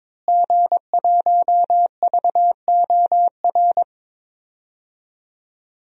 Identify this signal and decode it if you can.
Morse code, heard as a keyed tone, about 22 wpm, 700 Hz. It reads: G1VOR
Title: DXCC-100 Call Signs - 5 Characters - Repeated 2x - 22wpm